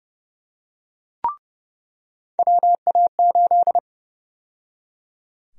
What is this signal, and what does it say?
Morse code, heard as a keyed tone, about 29 wpm, 700 Hz. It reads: WA8
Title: USA Call Sign Prefixes - 30wpm